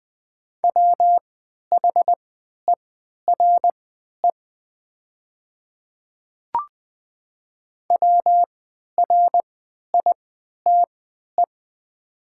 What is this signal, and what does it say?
Morse code, heard as a keyed tone, about 20 wpm, spaced as for 11 wpm, 700 Hz. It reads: WHERE WRITE
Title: 3x Character Spacing - Top 300 Words - 20wpm